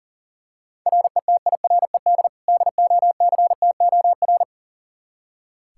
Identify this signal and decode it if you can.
Morse code, heard as a keyed tone, about 37 wpm, 700 Hz. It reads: RETIRED DOCTOR
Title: K1USN SST QSO Practice 40wpm